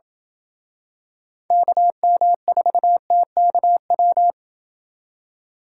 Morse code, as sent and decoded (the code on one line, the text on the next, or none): -.- -- ....- - -.- .--
KM4TKW